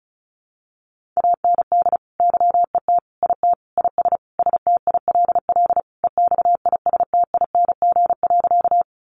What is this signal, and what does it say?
Morse code, heard as a keyed tone, about 34 wpm, 700 Hz. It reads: AND YET IT IS STILL EXISTING.
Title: Sentences from Top 1500 Words 35wpm